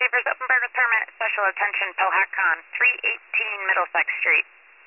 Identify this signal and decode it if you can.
Emergency Alert System (SAME) tones, heard as harsh buzzing bursts, with speech. It is heard from 0.7 to 4.4 s.